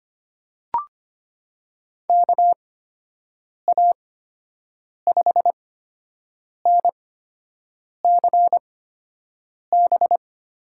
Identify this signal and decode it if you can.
Morse code, heard as a keyed tone, about 25 wpm, spaced as for 7 wpm, 700 Hz. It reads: KA5NCB
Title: USA Call Signs - 6 Characters - 8x Character Spacing - 25wpm